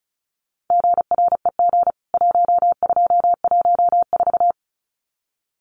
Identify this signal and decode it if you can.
Morse code, heard as a keyed tone, about 34 wpm, 700 Hz. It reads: GREG 1214
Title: CWT Exchange 35wpm